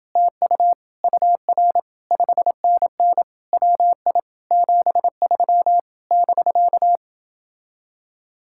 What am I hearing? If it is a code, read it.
Morse code, heard as a keyed tone, about 30 wpm, 700 Hz. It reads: TU UR 5NN WI 73 <BK>